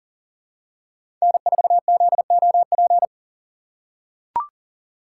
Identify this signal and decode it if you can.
Morse code, heard as a keyed tone, about 40 wpm, 700 Hz. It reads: N4ZOP
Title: USA Call Signs - 5 Characters - 3x Character Spacing - 40wpm